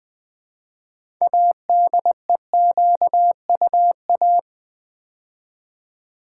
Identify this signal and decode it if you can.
Morse code, heard as a keyed tone, about 20 wpm, 700 Hz. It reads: ADEQUA
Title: Build A Word - 3 Letters At A Time - 8-Letter Words - 20wpm